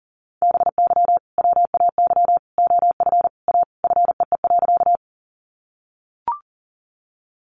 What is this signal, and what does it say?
Morse code, heard as a keyed tone, about 40 wpm, 700 Hz. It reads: BY WAY OF A FEE.
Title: Sentences from Top 1500 Words 40wpm